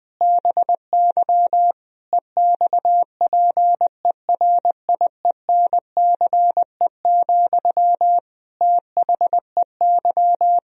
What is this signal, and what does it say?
Morse code, heard as a keyed tone, about 20 wpm, 700 Hz. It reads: BY EXPERIENCE, THEY
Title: Sentences from Top 1000 Words - Review Part 3 - 20wpm